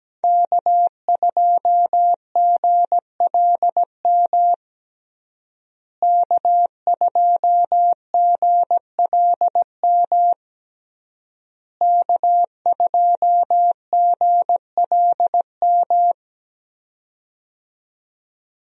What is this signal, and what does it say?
Morse code, heard as a keyed tone, about 17 wpm, 700 Hz. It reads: K2GLM K2GLM K2GLM
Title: USA Call Signs - 5 Characters - Repeated 3x - 17wpm